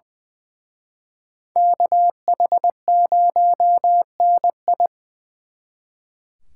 Morse code, heard as a keyed tone, about 24 wpm, 700 Hz. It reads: KH0NI
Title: Speed Racing - USA 2x2 Call Signs 20wpm